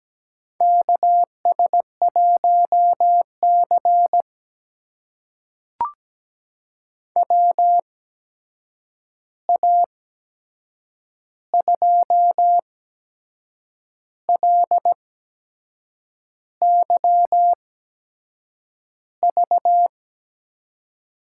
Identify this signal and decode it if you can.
Morse code, heard as a keyed tone, about 17 wpm, spaced as for 6 wpm, 700 Hz. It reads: KS1C WA2LYV
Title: USA Call Signs - 8x Character Spacing - 17wpm